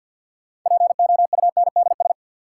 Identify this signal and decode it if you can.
Morse code, heard as a keyed tone, about 53 wpm, 700 Hz. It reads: POUNDS
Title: Sentences from Top 1000 Words - Review Part 2 - New Word Drill - 50wpm